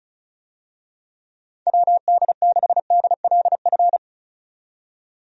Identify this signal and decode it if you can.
Morse code, heard as a keyed tone, about 34 wpm, 700 Hz. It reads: WD6DLF
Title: USA Call Signs - 6 Characters - 35wpm